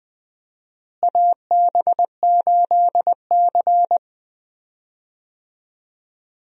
Morse code, heard as a keyed tone, about 20 wpm, 700 Hz. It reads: AB8C
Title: USA Call Signs - 4 Characters - Repeated 2x - 20wpm